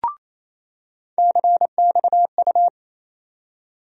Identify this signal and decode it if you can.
Morse code, heard as a keyed tone, about 28 wpm, 700 Hz. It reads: CXU
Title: USA Call Sign Postfixes - 3 Letters 28wpm